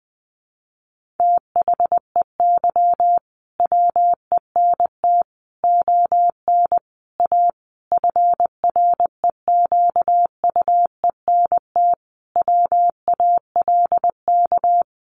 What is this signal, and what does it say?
Morse code, heard as a keyed tone, about 20 wpm, 700 Hz. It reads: THEY WENT ON A FREQUENT WALK
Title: Sentences from Top 1400 Words 20wpm